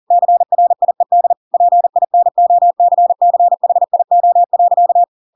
Morse code, heard as a keyed tone, about 40 wpm, 700 Hz. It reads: CRIED PINOCCHIO.
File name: Pinocchio / Pinocchio ch4 40 40 ik1hge